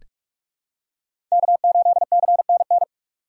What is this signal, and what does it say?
Morse code, heard as a keyed tone, about 44 wpm, 700 Hz. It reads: K8CNN
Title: USA Call Signs - 5 Characters - 3x Character Spacing - 45wpm